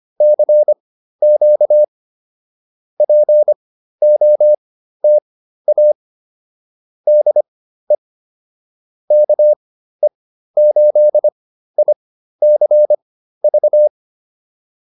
Morse code, heard as a keyed tone, about 25 wpm, spaced as for 13 wpm, 600 Hz. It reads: CQ POTA DE KE8ICV